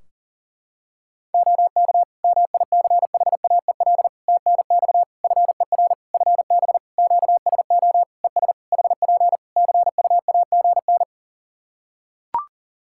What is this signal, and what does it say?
Morse code, heard as a keyed tone, about 40 wpm, 700 Hz. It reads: OK MICHAEL TNX FER FB QSO ES HP CUAGN
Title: Practice QSO - Exchange 4 - Ending - 40wpm